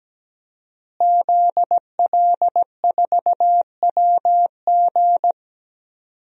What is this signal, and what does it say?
Morse code, heard as a keyed tone, about 17 wpm, 700 Hz. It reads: ZL4WG
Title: DXCC-100 Call Signs - 5 Characters - Repeated 3x - 17wpm